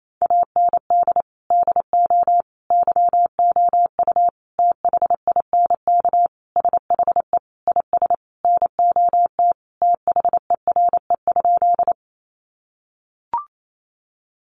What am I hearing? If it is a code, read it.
Morse code, heard as a keyed tone, about 28 wpm, 700 Hz. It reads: AND DO YOU THINK SHE IS NOT THERE?
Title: Sentences from Top 600 Words 28wpm